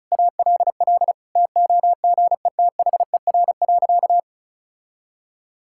Morse code, heard as a keyed tone, about 36 wpm, 700 Hz. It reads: ALL TOGETHER.
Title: Sentences from Top 1000 Words - Review Part 6 - 35wpm